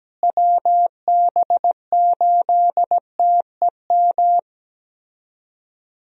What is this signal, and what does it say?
Morse code, heard as a keyed tone, about 17 wpm, 700 Hz. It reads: WB8TEM
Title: USA Call Signs - 6 Characters - Repeated 3x - 17wpm